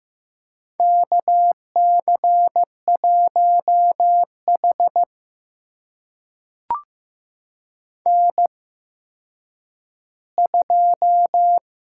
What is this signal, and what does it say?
Morse code, heard as a keyed tone, about 15 wpm, spaced as for 9 wpm, 700 Hz. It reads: KC1H N2
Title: USA Call Signs - 4 Characters - 8x Character Spacing - 15wpm